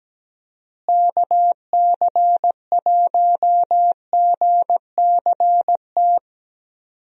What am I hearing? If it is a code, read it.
Morse code, heard as a keyed tone, about 17 wpm, 700 Hz. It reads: KC1GCT
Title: USA Call Signs - 6 Characters - Repeated 3x - 17wpm